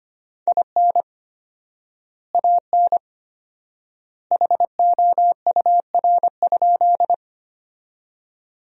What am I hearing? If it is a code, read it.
Morse code, heard as a keyed tone, about 25 wpm, 700 Hz. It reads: IN AN HOUR?
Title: 4x Word-Spacing - Sentences from Top 400 Words - 25wpm